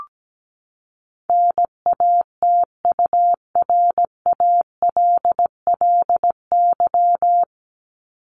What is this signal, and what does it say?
Morse code, heard as a keyed tone, about 17 wpm, 700 Hz. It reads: NATURALLY
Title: Top 1001-1250 Words 17wpm